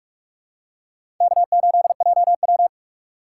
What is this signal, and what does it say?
Morse code, heard as a keyed tone, about 46 wpm, 700 Hz. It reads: K8JW
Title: USA Call Signs - 4 Characters - 6x Character Spacing - 45wpm